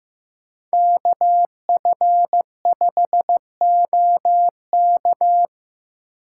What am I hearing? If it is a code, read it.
Morse code, heard as a keyed tone, about 18 wpm, 700 Hz. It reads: KF5OK
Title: Speed Racing - USA 2x2 Call Signs 15wpm